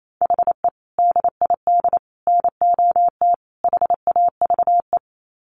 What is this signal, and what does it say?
Morse code, heard as a keyed tone, about 28 wpm, 700 Hz. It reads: HE DID NOT HAVE
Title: Sets of 4 Words 28wpm